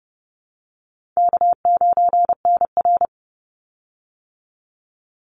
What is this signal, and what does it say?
Morse code, heard as a keyed tone, about 30 wpm, 700 Hz. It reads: K9NR
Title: Easy Call Signs - 30wpm